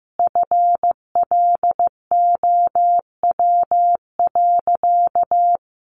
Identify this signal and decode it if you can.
Morse code, heard as a keyed tone, about 15 wpm, 700 Hz. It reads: FLOW.
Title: Sentences from Top 1200 Words 15wpm